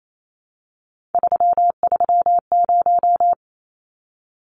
Morse code, heard as a keyed tone, about 28 wpm, 700 Hz. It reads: 330
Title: Numbers - 3-Digit 28wpm